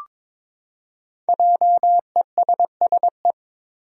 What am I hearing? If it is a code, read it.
Morse code, heard as a keyed tone, about 22 wpm, 700 Hz. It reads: JESSE
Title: USA Common Names - 22wpm